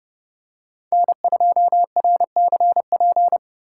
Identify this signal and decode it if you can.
Morse code, heard as a keyed tone, about 29 wpm, 700 Hz. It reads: N2RCP